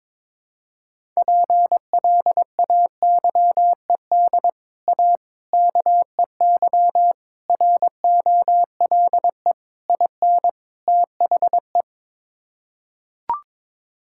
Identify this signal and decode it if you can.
Morse code, heard as a keyed tone, about 22 wpm, 700 Hz. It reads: PLAYED A KEY ROLE IN THE
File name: practice / 22wpm / Sets of 6 Words - 22wpm